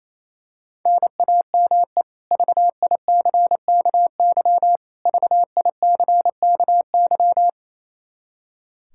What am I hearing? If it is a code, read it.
Morse code, heard as a keyed tone, about 28 wpm, 700 Hz. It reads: NAME VICKY VICKY
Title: K1USN SST QSO Practice 28wpm